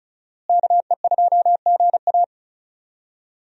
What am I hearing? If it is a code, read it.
Morse code, heard as a keyed tone, about 34 wpm, 700 Hz. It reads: KE2GA
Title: USA Call Signs - 5 Characters - Repeated 3x - 35wpm